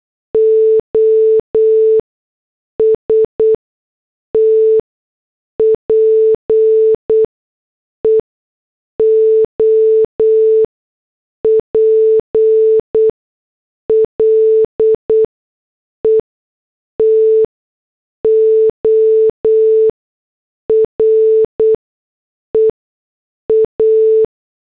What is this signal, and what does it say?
Morse code, heard as a keyed tone, about 8 wpm, spaced as for 6 wpm, 440 Hz. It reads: OST PEOPLE TO REA